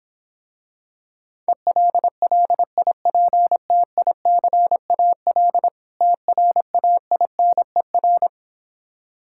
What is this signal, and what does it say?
Morse code, heard as a keyed tone, about 26 wpm, 700 Hz. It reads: ELLIPTICAL TRAINER